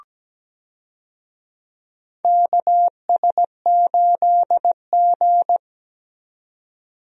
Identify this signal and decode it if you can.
Morse code, heard as a keyed tone, about 17 wpm, 700 Hz. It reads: KS8G
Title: USA Call Signs - 4 Characters - 17wpm